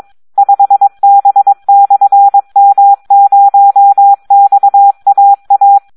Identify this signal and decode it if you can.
Morse code, heard as a keyed tone, about 22 wpm, 800 Hz. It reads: 5B/M0XAA
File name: Audio practica CW / SD 1 22